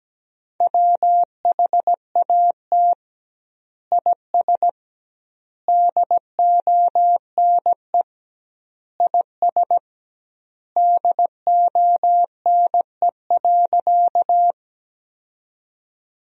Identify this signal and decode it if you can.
Morse code, heard as a keyed tone, about 17 wpm, 700 Hz. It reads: WHAT IS DONE IS DONE.
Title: 2x Word-Spacing - Sentences from Top 500 Words - 17wpm